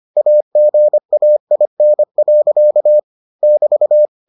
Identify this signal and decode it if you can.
Morse code, heard as a keyed tone, about 25 wpm, 600 Hz. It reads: AGAIN. <BT>